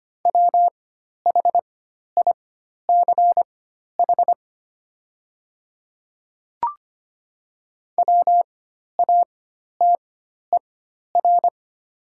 Morse code, heard as a keyed tone, about 25 wpm, spaced as for 12 wpm, 700 Hz. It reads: WHICH WATER
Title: 4x Character Spacing - Top 200 Words - 25wpm